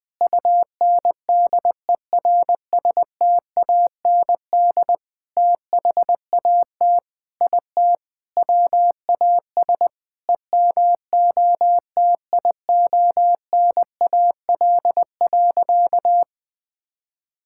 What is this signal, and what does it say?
Morse code, heard as a keyed tone, about 21 wpm, 700 Hz. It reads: UNDERSTAND THAT IT WAS EMOTIONAL.
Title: Sentences from Top 1000 Words - Review Part 5 - 20wpm